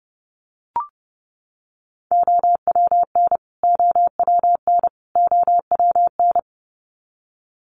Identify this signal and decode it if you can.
Morse code, heard as a keyed tone, about 30 wpm, 700 Hz. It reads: OWN OWN OWN
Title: Top 500 Words Repeated 30wpm